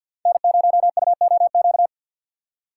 Morse code, heard as a keyed tone, about 53 wpm, 700 Hz. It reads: N0UOQ
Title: USA Call Signs - 5 Characters - 6x Character Spacing - 50wpm